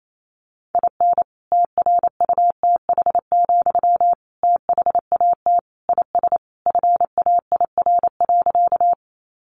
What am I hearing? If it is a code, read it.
Morse code, heard as a keyed tone, about 28 wpm, 700 Hz. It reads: IN TRUTH, THAT IS FAIR.
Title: Sentences from Top 1300 Words 28wpm